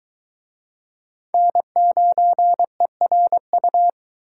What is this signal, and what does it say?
Morse code, heard as a keyed tone, about 21 wpm, 700 Hz. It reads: N9ERU